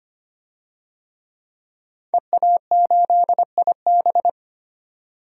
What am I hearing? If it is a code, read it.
Morse code, heard as a keyed tone, about 25 wpm, 700 Hz. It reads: EA8IB
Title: DXCC-100 Call Signs - 5 Characters - 25wpm